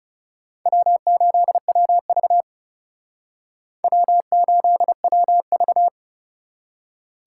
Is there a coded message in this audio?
Yes: Morse code from 0.7 to 5.9 s, heard as a keyed tone, about 38 wpm, 700 Hz. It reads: W8WV W8WV